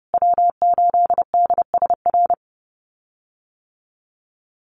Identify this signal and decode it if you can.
Morse code, heard as a keyed tone, about 30 wpm, 700 Hz. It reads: W8DSR
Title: USA Call Signs Repeated 3x 30wpm